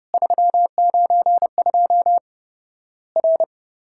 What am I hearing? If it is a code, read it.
Morse code, heard as a keyed tone, about 30 wpm, 700 Hz. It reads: 392 R
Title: Worked All Europe DX Contest - QTC Exchange - 30wpm